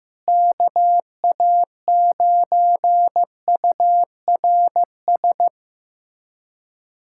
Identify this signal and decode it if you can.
Morse code, heard as a keyed tone, about 16 wpm, 700 Hz. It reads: KA9URS
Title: USA Callsign DE Callsign 15wpm